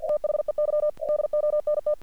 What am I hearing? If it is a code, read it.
Morse code, heard as a keyed tone, about 48 wpm, 600 Hz. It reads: THEY DONT
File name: archived / CRSnet